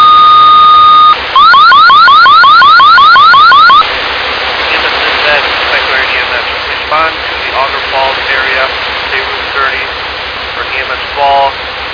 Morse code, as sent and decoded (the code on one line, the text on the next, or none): none